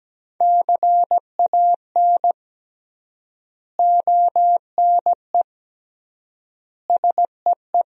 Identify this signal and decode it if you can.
Morse code, heard as a keyed tone, about 17 wpm, 700 Hz. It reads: CAN ONE SEE
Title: 3x Word-Spacing - Sentences from Top 100 Words - 17wpm